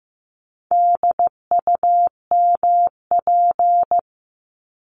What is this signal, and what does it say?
Morse code, heard as a keyed tone, about 15 wpm, 700 Hz. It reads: DUMP
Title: Top 1001-1250 Words 15wpm